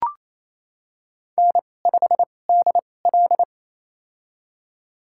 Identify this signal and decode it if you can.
Morse code, heard as a keyed tone, about 28 wpm, spaced as for 20 wpm, 700 Hz. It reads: N5DL
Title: USA Call Signs - 2x Character Spacing - 28wpm